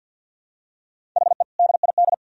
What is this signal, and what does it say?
Morse code, heard as a keyed tone, about 52 wpm, 700 Hz. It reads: HE DID
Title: Sentences from Top 1000 Words - Review Part 1 - 50wpm